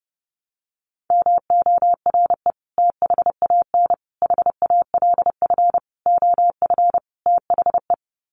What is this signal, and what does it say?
Morse code, heard as a keyed tone, about 30 wpm, 700 Hz. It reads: MORE THAN HALF OF THE
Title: Sets of 5 Words 30wpm